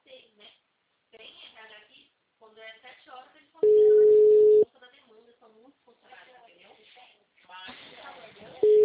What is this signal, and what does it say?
Ringback tone (Europe, 425 Hz), with speech.